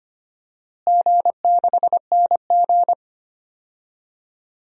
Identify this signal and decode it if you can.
Morse code, heard as a keyed tone, about 25 wpm, 700 Hz. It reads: G6NG
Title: DXCC-100 Call Signs - 4 Characters - Repeated 3x - 25wpm